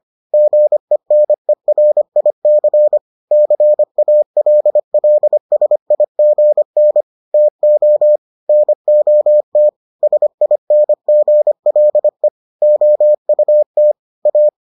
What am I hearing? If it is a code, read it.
Morse code, heard as a keyed tone, about 25 wpm, 600 Hz. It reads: GENERIC CALLSIGN TO NOT SINGLE OUT A